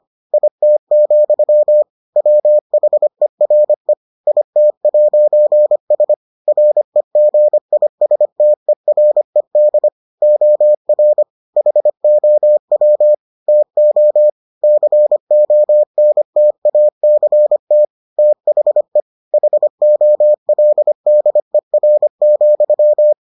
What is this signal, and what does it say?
Morse code, heard as a keyed tone, about 25 wpm, 600 Hz. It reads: IT, WHERE IT'S REGISTERED OR HOW TO CONTACT THE HOLDER,